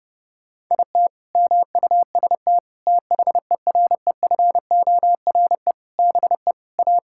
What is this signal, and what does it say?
Morse code, heard as a keyed tone, about 30 wpm, 700 Hz. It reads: IT MUST THEREFORE BE A
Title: Sentences from Top 1000 Words - Review Part 6 - 30wpm